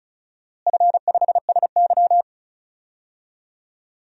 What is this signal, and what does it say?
Morse code, heard as a keyed tone, about 36 wpm, 700 Hz. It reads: F5SY
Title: DXCC-100 Call Signs - 4 Characters - Repeated 3x - 35wpm